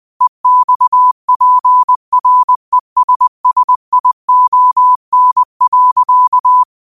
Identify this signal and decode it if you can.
Morse code, heard as a keyed tone, about 20 wpm, 1000 Hz. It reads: EXPRESSION.
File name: qotdfiles / 20wpm